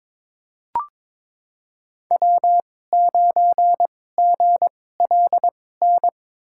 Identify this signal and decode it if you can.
Morse code, heard as a keyed tone, about 22 wpm, spaced as for 16 wpm, 700 Hz. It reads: W9GLN
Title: USA Call Signs - 5 Characters - 2x Character Spacing - 22wpm